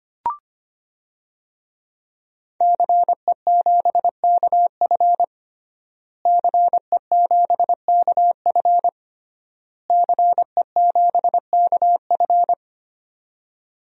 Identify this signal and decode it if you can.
Morse code, heard as a keyed tone, about 25 wpm, 700 Hz. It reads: CE7KF CE7KF CE7KF
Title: DXCC-100 Call Signs - 5 Characters - Repeated 3x - 25wpm